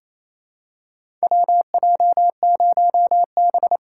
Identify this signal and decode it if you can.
Morse code, heard as a keyed tone, about 28 wpm, 700 Hz. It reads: WJ0B
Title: USA Call Signs - 6x Character Spacing - 28wpm